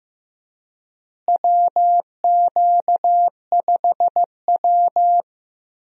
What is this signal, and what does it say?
Morse code, heard as a keyed tone, about 15 wpm, 700 Hz. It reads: WQ5W
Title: USA Call Signs - 4 Characters - Repeated 2x - 15wpm